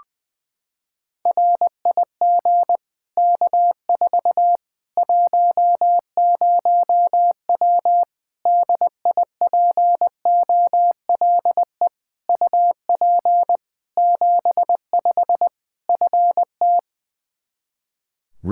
Morse code, heard as a keyed tone, about 21 wpm, 700 Hz. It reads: RIG K4 10W DIPOLE UP 75 FT
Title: Practice QSO - Exchange 2 - Info - 20wpm